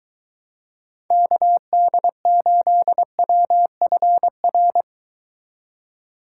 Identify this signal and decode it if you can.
Morse code, heard as a keyed tone, about 23 wpm, 700 Hz. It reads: KD8WFR